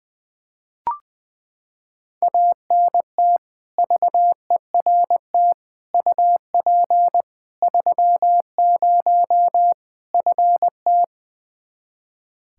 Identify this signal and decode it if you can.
Morse code, heard as a keyed tone, about 20 wpm, 700 Hz. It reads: ANT VERT UP 30 FT
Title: K1USN SST QSO Practice 20wpm